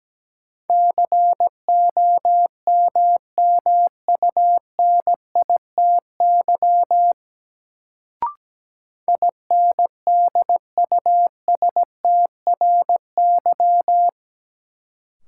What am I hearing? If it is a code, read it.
Morse code, heard as a keyed tone, about 17 wpm, 700 Hz. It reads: COMMUNITY INDUSTRY
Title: Sentences from Top 1000 Words - Review Part 4 - New Word Drill - 17wpm